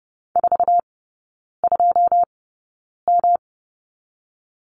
Morse code, heard as a keyed tone, about 30 wpm, 700 Hz. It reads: 4 2 M
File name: Lesson 19e - Letter w - Three Characters 30wpm